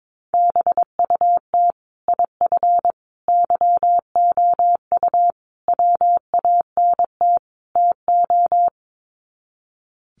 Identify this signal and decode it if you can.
Morse code, heard as a keyed tone, about 22 wpm, 700 Hz. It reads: BUT IF YOU WANT TO
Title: Sets of 5 Words Encore2 22wpm